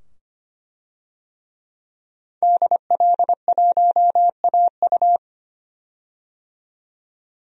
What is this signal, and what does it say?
Morse code, heard as a keyed tone, about 25 wpm, 700 Hz. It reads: DL1AU
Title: DXCC-100 Call Signs - 5 Characters - 25wpm